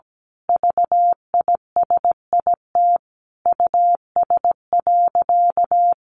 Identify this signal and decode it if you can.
Morse code, heard as a keyed tone, about 17 wpm, 700 Hz. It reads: VISIT US.
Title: Sentences from Top 1400 Words 17wpm